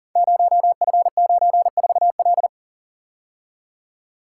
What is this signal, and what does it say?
Morse code, heard as a keyed tone, about 40 wpm, 700 Hz. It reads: 0F94L
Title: Code Groups - 5 Letter:Numbers - 40wpm